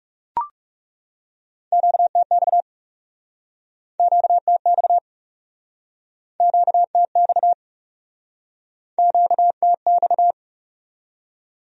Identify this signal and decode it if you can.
Morse code, heard as a keyed tone, about 42 wpm, 700 Hz. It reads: QTX QTX QTX QTX